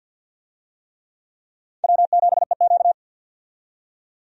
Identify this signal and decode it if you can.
Morse code, heard as a keyed tone, about 53 wpm, 700 Hz. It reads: W7EQ
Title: USA Call Signs - 4 Characters - 50wpm